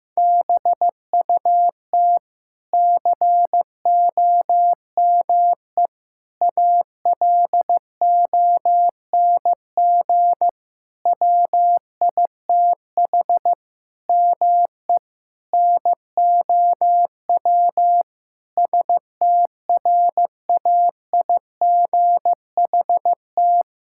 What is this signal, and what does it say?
Morse code, heard as a keyed tone, about 15 wpm, 700 Hz. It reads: BUT COME ALONG WITH ME NOW STRAIGHT
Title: Sentences from Top 1000 Words - Review Part 6 - 15wpm